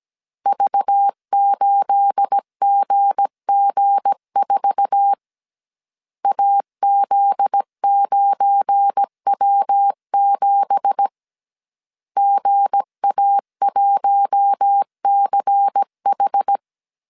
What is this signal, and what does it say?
Morse code, heard as a keyed tone, about 17 wpm, 800 Hz. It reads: V8GG4 AZ9W7 GA1CH
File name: CW / DET / 1 9 1500C S A 16wpm 2828